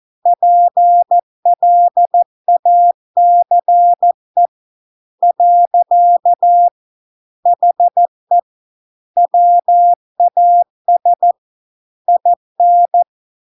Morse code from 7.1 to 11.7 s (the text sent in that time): HE WAS